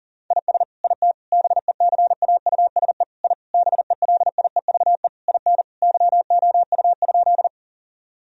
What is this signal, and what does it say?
Morse code, heard as a keyed tone, about 40 wpm, 700 Hz. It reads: IS IT BECAUSE I BELIEVE IN YOU?
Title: Sentences from Top 1000 Words - Review Part 3 - 40wpm